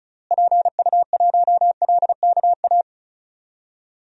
Morse code, heard as a keyed tone, about 34 wpm, 700 Hz. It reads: PU1LKA
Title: DXCC-100 Call Signs - 6 Characters - Repeated 3x - 35wpm